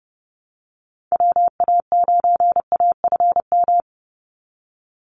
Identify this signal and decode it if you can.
Morse code, heard as a keyed tone, about 30 wpm, 700 Hz. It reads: WA9AFM
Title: USA Call Signs 30wpm